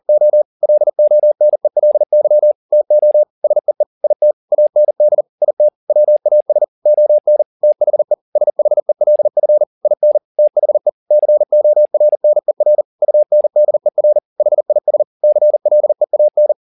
Morse code, heard as a keyed tone, about 40 wpm, 600 Hz. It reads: O LONELY TO SEE IT AND IT WAS ON THE SHELF IN THE CORNER UNDER HIS CLEAN